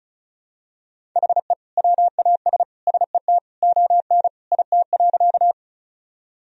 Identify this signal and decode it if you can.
Morse code, heard as a keyed tone, about 34 wpm, 700 Hz. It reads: HE WAS SET ON IT.
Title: Sentences from Top 1000 Words - Review Part 1 - 35wpm